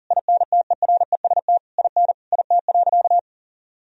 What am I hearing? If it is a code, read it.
Morse code, heard as a keyed tone, about 40 wpm, 700 Hz. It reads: INTEREST IN IT.
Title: Sentences from Top 1000 Words - Review Part 3 - 40wpm